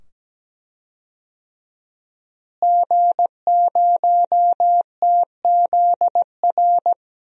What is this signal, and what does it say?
Morse code, heard as a keyed tone, about 17 wpm, 700 Hz. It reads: G0TZR
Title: DXCC-100 Call Signs - 5 Characters - Repeated 2x - 17wpm